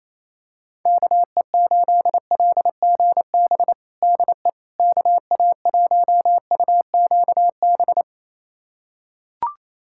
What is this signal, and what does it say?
Morse code, heard as a keyed tone, about 28 wpm, 700 Hz. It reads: KE8LGB DE KA1UQB
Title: USA Callsign DE Callsign - 6 Characters - 28wpm